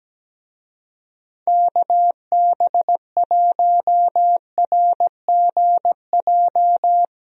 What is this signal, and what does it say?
Morse code, heard as a keyed tone, about 17 wpm, 700 Hz. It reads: KB1RGJ
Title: USA Call Signs - 6 Characters - 17wpm